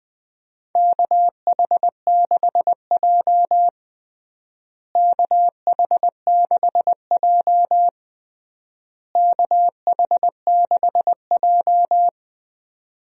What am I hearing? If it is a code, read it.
Morse code, heard as a keyed tone, about 20 wpm, 700 Hz. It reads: KH6J KH6J KH6J
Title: USA Call Signs - 4 Characters - Repeated 3x - 20wpm